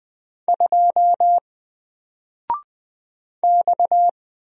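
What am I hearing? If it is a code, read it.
Morse code, heard as a keyed tone, about 20 wpm, 700 Hz. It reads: 2 X
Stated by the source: Single letter and number Morse drill at 20 wpm, from N4LCD.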